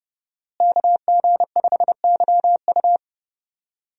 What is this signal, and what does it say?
Morse code, heard as a keyed tone, about 30 wpm, 700 Hz. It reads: KG5YU
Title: USA Call Signs - 5 Characters - Repeated 2x - 30wpm